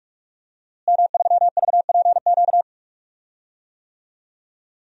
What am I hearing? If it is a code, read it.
Morse code, heard as a keyed tone, about 46 wpm, 700 Hz. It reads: M3VPQ
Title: DXCC-100 Call Signs - 5 Characters - Repeated 3x - 45wpm